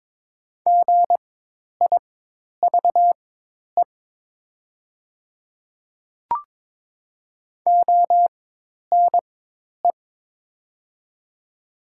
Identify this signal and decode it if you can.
Morse code, heard as a keyed tone, about 22 wpm, spaced as for 10 wpm, 700 Hz. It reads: GIVE ONE
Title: 4x Character Spacing - Top 500 Words - 22wpm